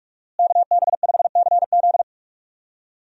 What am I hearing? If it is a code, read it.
Morse code, heard as a keyed tone, about 48 wpm, 700 Hz. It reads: KB5CZ
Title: USA Call Signs - 5 Characters - Repeated 2x - 45wpm